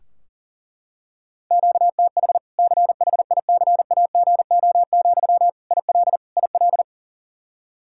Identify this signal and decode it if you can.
Morse code, heard as a keyed tone, about 40 wpm, 700 Hz. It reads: QTH CHICAGO, IL IL
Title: Practice QSO - Exchange 1 - Beginning - 40wpm